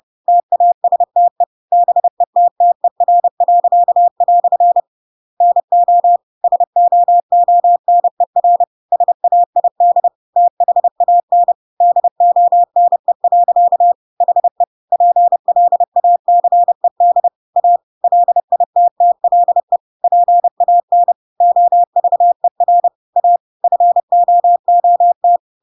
Morse code, heard as a keyed tone, about 30 wpm, 700 Hz. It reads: TASTE BETTER." NO SOONER SAID THAN DONE. HE PLACED A LITTLE PAN OVER A FOOT
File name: Pinocchio / Pinocchio ch5 30 30 ik1hge